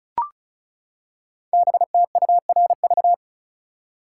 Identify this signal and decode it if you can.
Morse code, heard as a keyed tone, about 36 wpm, 700 Hz. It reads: BTURV
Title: Code Groups - 5 Letters - 35wpm